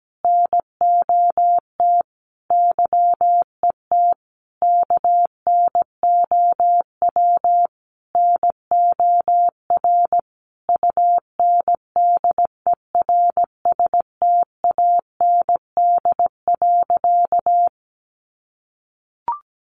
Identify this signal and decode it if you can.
Morse code, heard as a keyed tone, about 17 wpm, 700 Hz. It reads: NOT YET KNOW NOR UNDERSTAND.
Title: Sentences from Top 1500 Words 17wpm